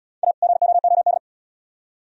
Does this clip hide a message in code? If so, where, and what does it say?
Yes: Morse code from 0.2 to 1.2 s, heard as a keyed tone, about 69 wpm, 700 Hz. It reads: A GOOD